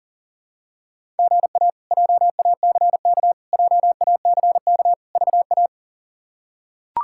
Morse code, heard as a keyed tone, about 40 wpm, 700 Hz. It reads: GA JACK JACK VA
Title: K1USN SST QSO Practice - 40wpm